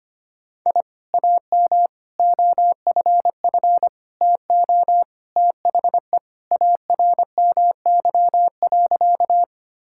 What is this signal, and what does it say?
Morse code, heard as a keyed tone, about 25 wpm, 700 Hz. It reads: I AM OFF TO THE ARMY.
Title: Sentences from Top 1000 Words - Review Part 5 - 25wpm